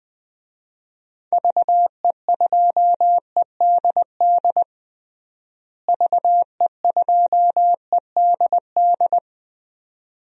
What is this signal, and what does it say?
Morse code, heard as a keyed tone, about 20 wpm, 700 Hz. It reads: VE2EDD VE2EDD
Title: DXCC-100 Call Signs - 6 Characters - Repeated 2x - 20wpm